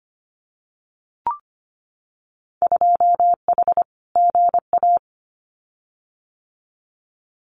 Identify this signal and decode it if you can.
Morse code, heard as a keyed tone, about 25 wpm, 690 Hz. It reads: 2H GA
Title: Winter Field Day Exchanges Encore 25wpm